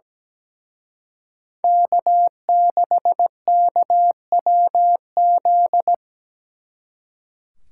Morse code, heard as a keyed tone, about 20 wpm, 700 Hz. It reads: K6KWZ
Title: Speed Racing - USA 1x3 Call Signs 17wpm